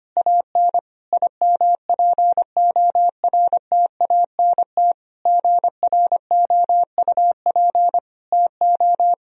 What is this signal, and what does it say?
Morse code, heard as a keyed tone, about 25 wpm, 700 Hz. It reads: AN IMPORTANT GROUP TO
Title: Sentences from Top 1000 Words - Review Part 1 - 25wpm